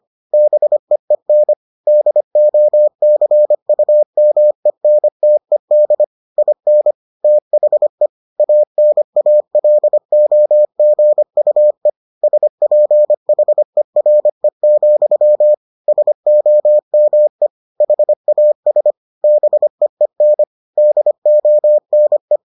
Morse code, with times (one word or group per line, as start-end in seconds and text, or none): 0.3-1.5 BEEN
1.9-6.0 DOCUMENTED
6.4-6.9 IN
7.2-8.1 THE
8.4-11.9 ANALOGUE
12.2-15.5 SPHERE,
15.9-17.5 SOME
17.8-18.9 HAS
19.2-20.4 BEEN
20.8-22.4 DONE